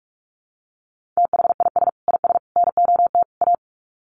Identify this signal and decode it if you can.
Morse code, heard as a keyed tone, about 44 wpm, 700 Hz. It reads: THIS IS NOT A
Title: Sets of 4 Words 45wpm